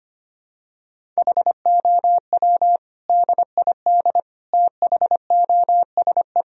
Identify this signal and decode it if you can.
Morse code, heard as a keyed tone, about 25 wpm, 700 Hz. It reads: HOW DID THOSE
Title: Sentences from Top 1000 Words - Review Part 1 - 25wpm